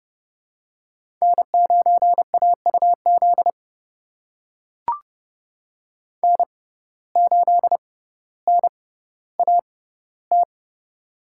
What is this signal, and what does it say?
Morse code, heard as a keyed tone, about 30 wpm, spaced as for 15 wpm, 700 Hz. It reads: N9AUZ N8NAT
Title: USA Call Signs - 5 Characters - 6x Character Spacing - 30wpm